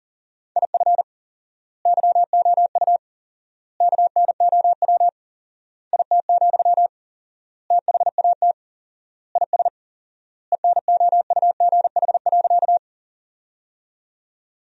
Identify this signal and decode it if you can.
Morse code, heard as a keyed tone, about 40 wpm, 700 Hz. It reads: IF YOU KNOW IT, THAT IS ENOUGH.
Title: 4x Word-Spacing - Sentences from Top 400 Words - 40wpm